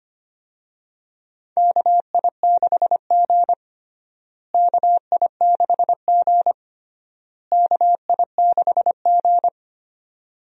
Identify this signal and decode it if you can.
Morse code, heard as a keyed tone, about 25 wpm, 700 Hz. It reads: KI6G KI6G KI6G
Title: USA Call Signs - 4 Characters - Repeated 3x - 25wpm